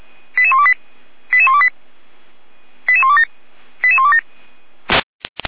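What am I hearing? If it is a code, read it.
Five-tone selective call (ZVEI), heard as quick runs of five tones, tones of about 70 ms. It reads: 88128 88128 88127 88127
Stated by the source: Five-tone ZVEI selective call 88112 received by SDR, channel 455.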